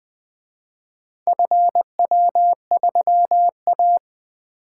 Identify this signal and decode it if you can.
Morse code, heard as a keyed tone, about 24 wpm, 700 Hz. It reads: FW3A
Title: Speed-Racing - Four Letter-Number Combinations - 20wpm